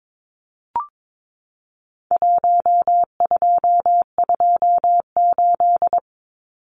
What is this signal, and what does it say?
Morse code, heard as a keyed tone, about 22 wpm, 700 Hz. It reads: 1228
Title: Numbers - Contest Style - 4-Digit 22wpm